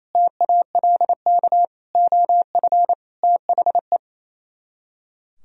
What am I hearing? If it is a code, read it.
Morse code, heard as a keyed tone, about 35 wpm, 700 Hz. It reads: TALK OF THE
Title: Speed-Racing - Sets of 5 Words 28wpm